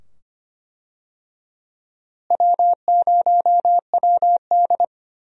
Morse code, heard as a keyed tone, about 25 wpm, 700 Hz. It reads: W0WD
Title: USA Call Signs - 4 Characters - Repeated 2x - 25wpm